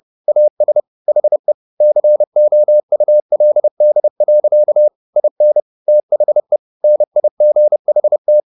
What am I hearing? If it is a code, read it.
Morse code, heard as a keyed tone, about 30 wpm, 600 Hz. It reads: AS HE COULD. IN THE NIGHT